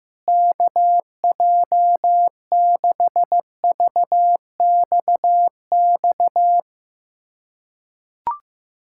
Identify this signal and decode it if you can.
Morse code, heard as a keyed tone, about 15 wpm, 700 Hz. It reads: KJ6VXX
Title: USA Callsign DE Callsign - 6 Characters - 15wpm